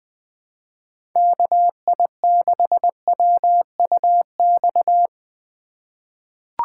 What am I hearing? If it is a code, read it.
Morse code, heard as a keyed tone, about 20 wpm, 700 Hz. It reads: KI6WUX
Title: USA Call Signs - 4x Character Spacing - 20wpm